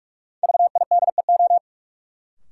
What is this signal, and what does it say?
Morse code, heard as a keyed tone, about 45 wpm, 700 Hz. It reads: VIDEO
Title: Sentences from Top 1000 Words - Review Part 4 - New Word Drill - 45wpm